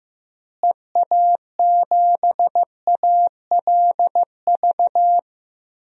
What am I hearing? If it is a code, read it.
Morse code, heard as a keyed tone, about 15 wpm, 700 Hz. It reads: EA7ALV
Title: DXCC-100 Call Signs - 6 Characters - Repeated 3x - 15wpm